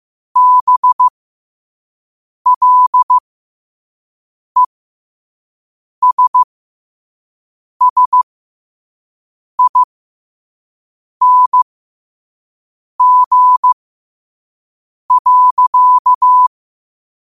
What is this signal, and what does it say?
Morse code, heard as a keyed tone, about 15 wpm, spaced as for 5 wpm, 1000 Hz. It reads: BLESSING.
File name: qotdfiles / 5wpm